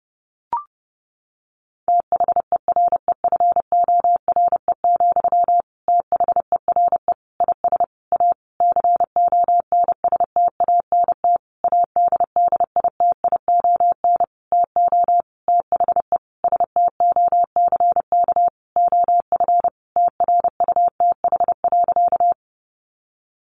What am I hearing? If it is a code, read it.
Morse code, heard as a keyed tone, about 30 wpm, 700 Hz. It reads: THEREFORE, THERE IS A CONSTANT ADDITION TO THE STOCK OF TRUTH.
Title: Sentences from Top 1500 Words 30wpm